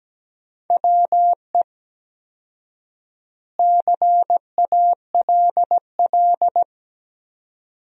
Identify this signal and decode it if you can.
Morse code, heard as a keyed tone, about 17 wpm, 700 Hz. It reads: WE CALL
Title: 4x Word-Spacing - Sentences from Top 400 Words - 17wpm